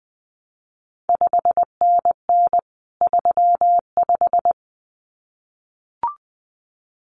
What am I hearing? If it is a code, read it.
Morse code, heard as a keyed tone, about 19 wpm, 700 Hz. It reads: 5NN 35
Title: CQ WW DX Contest Exchange - 20wpm